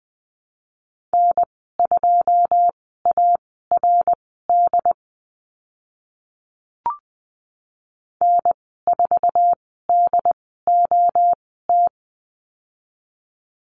Morse code, heard as a keyed tone, about 20 wpm, spaced as for 15 wpm, 700 Hz. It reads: N2ARD N4DOT
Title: 082 Lesson 14g - Number 2 - Call Signs 20wpm